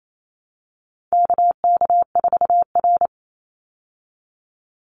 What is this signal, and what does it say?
Morse code, heard as a keyed tone, about 28 wpm, 700 Hz. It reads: KK4R
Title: Call Signs Repeated 2x - 28wpm